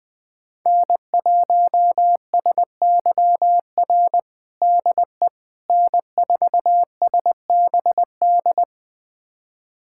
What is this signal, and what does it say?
Morse code, heard as a keyed tone, about 20 wpm, 700 Hz. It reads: N1SYR DE N4SBD
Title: USA Callsign DE Callsign - 5 Characters - 20wpm